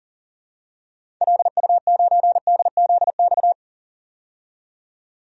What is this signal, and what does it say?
Morse code, heard as a keyed tone, about 40 wpm, 700 Hz. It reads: LU9DZX
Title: DXCC-100 Call Signs - 6 Characters - 40wpm